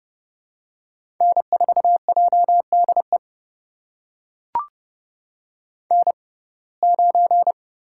Morse code, heard as a keyed tone, about 32 wpm, spaced as for 22 wpm, 700 Hz. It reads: N4JDE N9
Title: USA Call Signs - 5 Characters - 6x Character Spacing - 30wpm